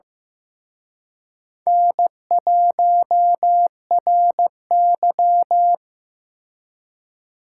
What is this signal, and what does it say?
Morse code, heard as a keyed tone, about 17 wpm, 700 Hz. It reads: N1RY
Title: Speed Racing - USA 1x2 Call Signs 15wpm